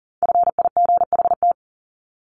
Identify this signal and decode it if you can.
Morse code, heard as a keyed tone, about 40 wpm, 700 Hz. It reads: FIGHT
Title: ING Suffix - Top 100 Words with Base Words 40wpm